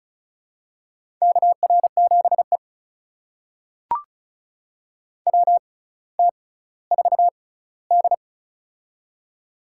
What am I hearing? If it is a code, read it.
Morse code, heard as a keyed tone, about 36 wpm, spaced as for 18 wpm, 700 Hz. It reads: KR7E WT4D
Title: USA Call Signs - 4 Characters - 6x Character Spacing - 35wpm